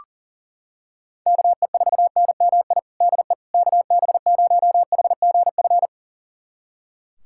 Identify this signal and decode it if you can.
Morse code, heard as a keyed tone, about 40 wpm, 700 Hz. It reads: KE4NMI DE KB0HGF
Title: USA Callsign DE Callsign - 6 Characters - 40wpm